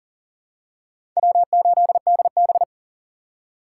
Morse code, heard as a keyed tone, about 40 wpm, 700 Hz. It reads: W8DB
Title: USA Call Signs - 2x Character Spacing - 40wpm